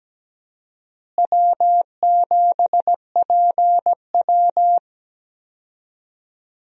Morse code, heard as a keyed tone, about 17 wpm, 700 Hz. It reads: W7PW
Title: USA Call Signs - 4 Characters - Repeated 3x - 17wpm